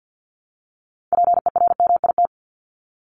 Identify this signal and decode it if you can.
Morse code, heard as a keyed tone, about 50 wpm, 700 Hz. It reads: PERMIT
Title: Top 1751-2000 Words 50wpm